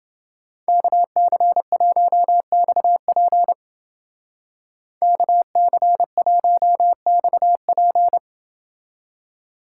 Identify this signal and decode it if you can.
Morse code, heard as a keyed tone, about 31 wpm, 700 Hz. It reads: KC1XP KC1XP